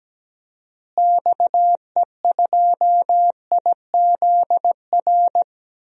Morse code, heard as a keyed tone, about 17 wpm, 700 Hz. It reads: XE2IZR
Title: DXCC-100 Call Signs - 6 Characters - Repeated 2x - 17wpm